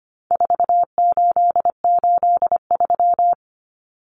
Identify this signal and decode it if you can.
Morse code, heard as a keyed tone, about 25 wpm, 700 Hz. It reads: 4883
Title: Numbers - 4-Digit 25wpm